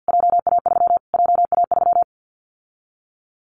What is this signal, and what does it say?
Morse code, heard as a keyed tone, about 50 wpm, 700 Hz. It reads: JA3 JA3
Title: Callsign Prefixes for Top 25 DXCC - Repeated 3x - 50wpm